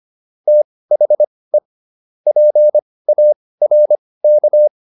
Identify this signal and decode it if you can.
Morse code, heard as a keyed tone, about 25 wpm, spaced as for 18 wpm, 600 Hz. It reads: THE PARK